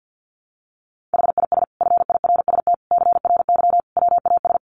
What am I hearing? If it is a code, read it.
Morse code, heard as a keyed tone, about 50 wpm, 700 Hz. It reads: HIS FIRST CRY WAS
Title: Sentences from Top 500 Words 50wpm